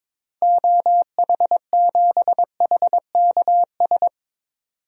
Morse code, heard as a keyed tone, about 22 wpm, 700 Hz. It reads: OH7HKS
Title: DXCC-100 Call Signs - 6 Characters - Repeated 3x - 22wpm